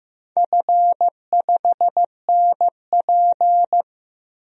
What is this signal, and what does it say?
Morse code, heard as a keyed tone, about 15 wpm, 700 Hz. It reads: F5NP
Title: DXCC-100 Call Signs - 4 Characters - Repeated 3x - 15wpm